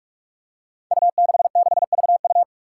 Morse code, heard as a keyed tone, about 41 wpm, 700 Hz. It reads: U66VU
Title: Five Challenging Letter-Number Combinations 45wpm